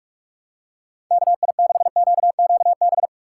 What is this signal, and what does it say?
Morse code, heard as a keyed tone, about 44 wpm, 700 Hz. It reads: KI6QQB
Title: USA Call Signs - 6 Characters - 6x Character Spacing - 45wpm